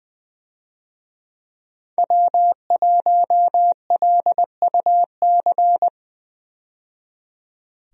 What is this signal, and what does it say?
Morse code, heard as a keyed tone, about 20 wpm, 700 Hz. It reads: W1LUC
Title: USA Call Signs - 5 Characters - 20wpm